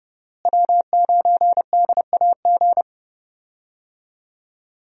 Morse code, heard as a keyed tone, about 30 wpm, 700 Hz. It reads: W9DAG
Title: USA Call Signs - 5 Characters - Repeated 3x - 30wpm